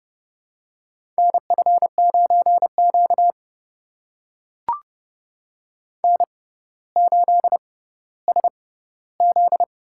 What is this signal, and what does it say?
Morse code, heard as a keyed tone, about 30 wpm, spaced as for 15 wpm, 700 Hz. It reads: NF9Q N8SZ
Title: USA Call Signs - 4 Characters - 6x Character Spacing - 30wpm